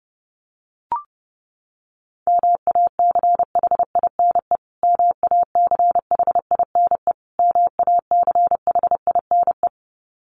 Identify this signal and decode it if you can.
Morse code, heard as a keyed tone, about 30 wpm, 700 Hz. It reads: MACHINE MACHINE MACHINE
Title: Top 500 Words Repeated 30wpm